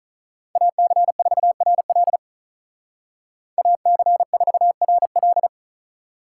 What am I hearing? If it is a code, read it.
Morse code, heard as a keyed tone, about 41 wpm, 700 Hz. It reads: AC4RL AC4RL
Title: Speed Racing - USA 2x2 Call Signs 35wpm